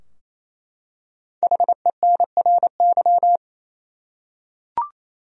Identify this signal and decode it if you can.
Morse code, heard as a keyed tone, about 28 wpm, 700 Hz. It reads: HENRY
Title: USA Common Names - 28wpm